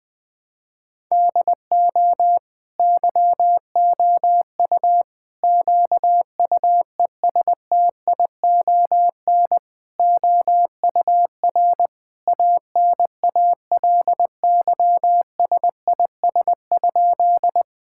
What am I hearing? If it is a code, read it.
Morse code, heard as a keyed tone, about 21 wpm, 700 Hz. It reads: DO YOU QUESTION OUR ANALYSIS?
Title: Sentences from Top 1000 Words - Review Part 5 - 20wpm